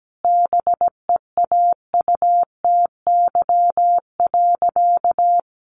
Morse code, heard as a keyed tone, about 17 wpm, 700 Hz. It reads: BEAUTY.
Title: Sentences from Top 1300 Words 17wpm